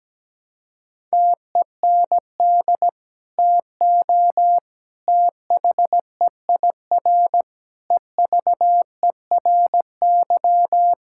Morse code, heard as a keyed tone, about 17 wpm, 700 Hz. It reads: TEND TO THEIR EVERY
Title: Sentences from Top 1000 Words - Review Part 5 - 17wpm